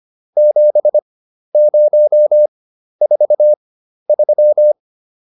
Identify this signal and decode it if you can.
Morse code, heard as a keyed tone, about 25 wpm, spaced as for 12 wpm, 600 Hz. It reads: 7043